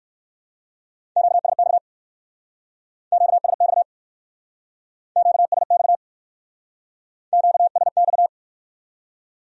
Morse code, heard as a keyed tone, about 58 wpm, 700 Hz. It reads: QSX QSX QSX QSX